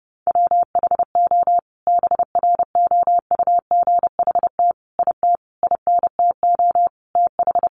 Morse code, heard as a keyed tone, about 30 wpm, 700 Hz. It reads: WHO BROUGHT IT INTO TH
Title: Sentences from Top 600 Words 30wpm